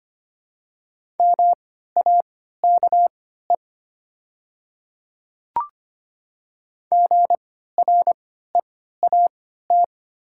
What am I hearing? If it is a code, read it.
Morse code, heard as a keyed tone, about 25 wpm, spaced as for 14 wpm, 700 Hz. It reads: MAKE GREAT
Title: 3x Character Spacing - Top 300 Words - 25wpm